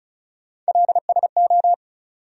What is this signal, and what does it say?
Morse code, heard as a keyed tone, about 36 wpm, 700 Hz. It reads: LSO
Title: USA Call Sign Postfixes - 3 Letters 35wpm